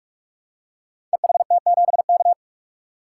Morse code, heard as a keyed tone, about 44 wpm, 700 Hz. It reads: EHT7K
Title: Code Groups - 5 Letter:Numbers - 45wpm